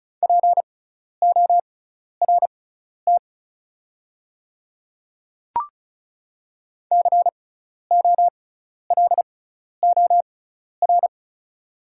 Morse code, heard as a keyed tone, about 34 wpm, spaced as for 12 wpm, 690 Hz. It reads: PORT COLOR
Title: 6x Character Spacing - Top 500 Words - 35wpm